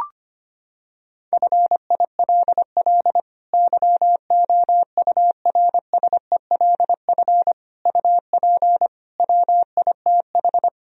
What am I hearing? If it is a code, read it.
Morse code, heard as a keyed tone, about 25 wpm, 700 Hz. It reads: FILL YOURSELF UP WITH
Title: Sentences from Top 1000 Words - Review Part 3 - 25wpm